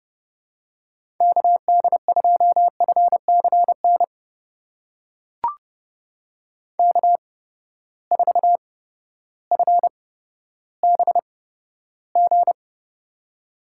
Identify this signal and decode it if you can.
Morse code, heard as a keyed tone, about 31 wpm, spaced as for 14 wpm, 700 Hz. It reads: KD2FCN K4FBG
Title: USA Call Signs - 8x Character Spacing - 30wpm